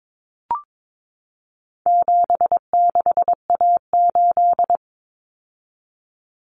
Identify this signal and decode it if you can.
Morse code, heard as a keyed tone, about 22 wpm, 700 Hz. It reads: 76A8
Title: Common Cut Numbers - 4-Digit 22wpm